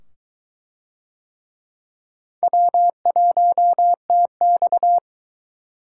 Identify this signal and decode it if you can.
Morse code, heard as a keyed tone, about 22 wpm, 700 Hz. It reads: W1TX